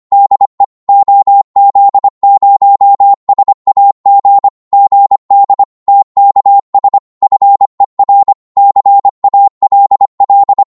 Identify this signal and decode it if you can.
Morse code, heard as a keyed tone, about 24 wpm, 800 Hz. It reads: DE OZ0SAG GD TKS FER CALL
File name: Morse / qso-25wpm